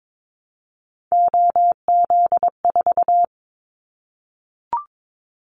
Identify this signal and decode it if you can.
Morse code, heard as a keyed tone, about 22 wpm, 700 Hz. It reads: OZ4
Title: Callsign Prefixes for Top 100 DXCC - 22wpm